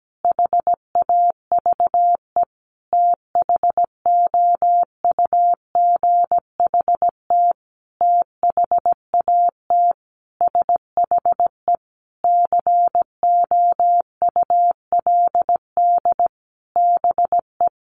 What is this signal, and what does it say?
Morse code, heard as a keyed tone, about 17 wpm, 700 Hz. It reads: HAVE THOUGHT THAT SHE COULD BE
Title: Sentences from Top 1200 Words 17wpm